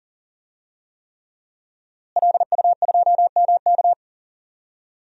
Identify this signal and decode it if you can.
Morse code, heard as a keyed tone, about 40 wpm, 700 Hz. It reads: LU2MK
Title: DXCC-100 Call Signs - 5 Characters - Repeated 3x - 40wpm